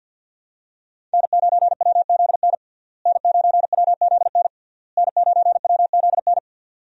Morse code, heard as a keyed tone, about 46 wpm, 700 Hz. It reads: N9WZN N9WZN N9WZN
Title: USA Call Signs - 5 Characters - Repeated 3x - 50wpm